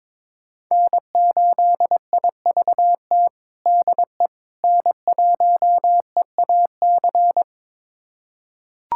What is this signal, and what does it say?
Morse code, heard as a keyed tone, about 22 wpm, 700 Hz. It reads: N8IVT DE N1EAC
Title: USA Callsign DE Callsign - 5 Characters - 22wpm